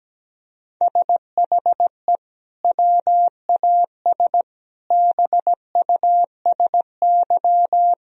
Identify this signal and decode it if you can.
Morse code, heard as a keyed tone, about 17 wpm, 700 Hz. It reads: SHE WAS BUSY
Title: Sentences from Top 1000 Words - Review Part 2 - 17wpm